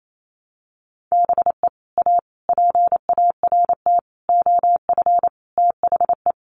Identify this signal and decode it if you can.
Morse code, heard as a keyed tone, about 28 wpm, 700 Hz. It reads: BE A PART OF THE
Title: Sets of 5 Words Encore 28wpm